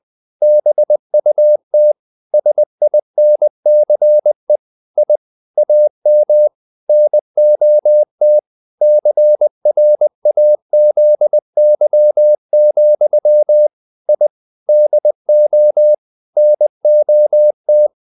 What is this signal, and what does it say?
Morse code, heard as a keyed tone, about 20 wpm, 600 Hz. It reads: BUT SINCE I AM NOT CRAZY, I DO NOT
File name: the old man and the sea / the old man and the sea part 1 20wpm